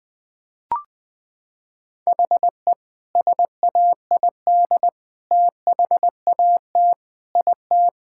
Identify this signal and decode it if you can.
Morse code, heard as a keyed tone, about 20 wpm, 700 Hz. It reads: HE SAID THAT IT
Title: Sentences from Top 1600 Words 20wpm